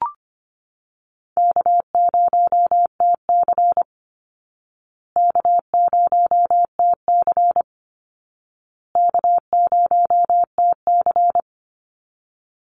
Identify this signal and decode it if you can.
Morse code, heard as a keyed tone, about 25 wpm, 700 Hz. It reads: K0TC K0TC K0TC
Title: Call Signs Repeated 2x - 25wpm